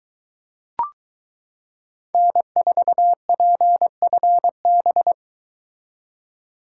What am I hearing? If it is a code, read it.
Morse code, heard as a keyed tone, about 22 wpm, 700 Hz. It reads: N4PFB